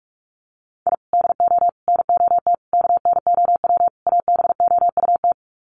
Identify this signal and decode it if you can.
Morse code, heard as a keyed tone, about 44 wpm, 700 Hz. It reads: I DO NOT KNOW ABOUT
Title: Sets of 5 Words 45wpm